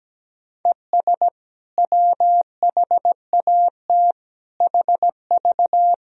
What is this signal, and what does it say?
Morse code, heard as a keyed tone, about 17 wpm, 700 Hz. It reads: ES WHAT HV
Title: Sentences from Top 500 Words and Abbreviations 17wpm